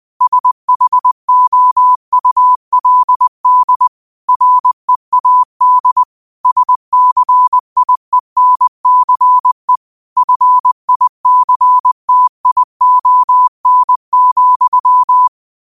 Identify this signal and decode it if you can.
Morse code, heard as a keyed tone, about 20 wpm, 1000 Hz. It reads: SHOULD READ SCIENCE FICTION,
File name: qotdfiles / 20wpm